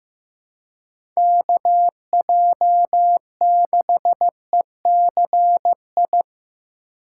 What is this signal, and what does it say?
Morse code, heard as a keyed tone, about 15 wpm, 700 Hz. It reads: KJ6ECI
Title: USA Call Signs - 6 Characters - 2x Character Spacing - 15wpm